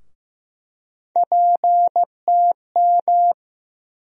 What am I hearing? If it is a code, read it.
Morse code, heard as a keyed tone, about 15 wpm, 700 Hz. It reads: PTM
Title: USA Call Sign Postfixes - 3 Letters 15wpm